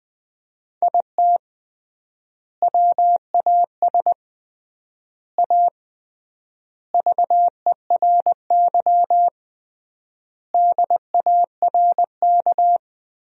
Morse code, heard as a keyed tone, about 21 wpm, 700 Hz. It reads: IT WAS A VERY DARK
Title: 3x Word-Spacing - Sentences from Top 500 Words - 20wpm